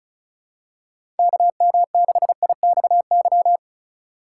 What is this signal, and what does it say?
Morse code, heard as a keyed tone, about 36 wpm, 700 Hz. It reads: KM6IXY
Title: USA Call Signs - 6 Characters - 4x Character Spacing - 35wpm